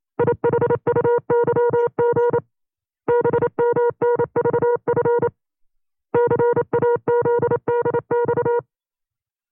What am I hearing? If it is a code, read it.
Morse code, heard as a keyed tone, about 28 wpm, 480 Hz. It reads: IHUYG BMNVF CAZDX
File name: CW-practice/25WPM